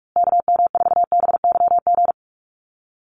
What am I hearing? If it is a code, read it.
Morse code, heard as a keyed tone, about 44 wpm, 700 Hz. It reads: KM4BYG
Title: USA Call Signs Repeated 3x 45wpm